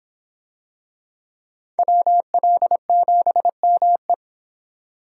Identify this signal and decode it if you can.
Morse code, heard as a keyed tone, about 24 wpm, 700 Hz. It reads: WL7ME